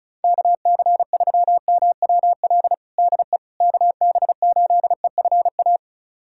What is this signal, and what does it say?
Morse code, heard as a keyed tone, about 34 wpm, 700 Hz. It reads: KC3MWL DE KB8EFA
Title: USA Callsign DE Callsign - 6 Characters - 35wpm